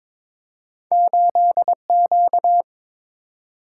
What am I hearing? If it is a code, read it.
Morse code, heard as a keyed tone, about 22 wpm, 700 Hz. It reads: 8Q
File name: practice / 22wpm / Two Letter-Number Combinations - 22wpm